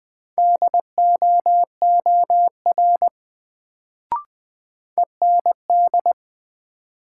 Morse code, heard as a keyed tone, about 20 wpm, 700 Hz. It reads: DOOR END
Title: Sets of 2 Words - New Word Drill - 20wpm